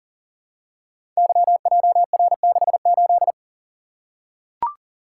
Morse code, heard as a keyed tone, about 40 wpm, 700 Hz. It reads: YJR68
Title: Code Groups - 5 Letter:Numbers - 40wpm